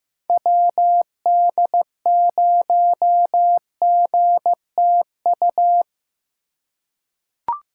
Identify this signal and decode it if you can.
Morse code, heard as a keyed tone, about 15 wpm, 700 Hz. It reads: WD0GTU
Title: USA Call Signs - 6 Characters - 3x Character Spacing - 15wpm